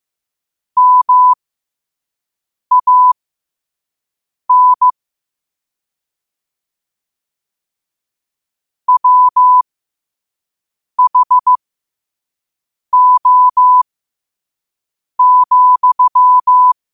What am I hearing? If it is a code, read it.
Morse code, heard as a keyed tone, about 15 wpm, spaced as for 5 wpm, 1000 Hz. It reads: MAN WHO,
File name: qotdfiles / 5wpm